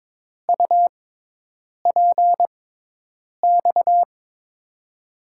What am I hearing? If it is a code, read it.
Morse code, heard as a keyed tone, about 22 wpm, spaced as for 8 wpm, 700 Hz. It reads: UPX
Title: USA Call Signs - 6x Character Spacing - 22wpm